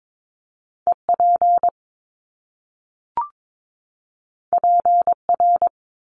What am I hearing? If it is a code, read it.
Morse code, heard as a keyed tone, about 22 wpm, 700 Hz. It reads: EP PR
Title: Top Two Letter Combinations in Words 22wpm